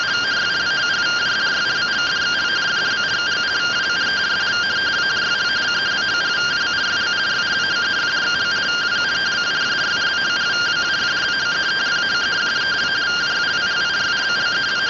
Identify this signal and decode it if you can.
Two-tone teleprinter signal, about 170 Hz shift.